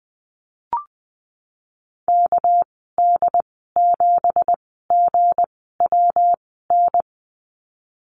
Morse code, heard as a keyed tone, about 20 wpm, spaced as for 15 wpm, 700 Hz. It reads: KD7GWN
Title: 200 Lesson 32g - Letter k - Call Signs 20wpm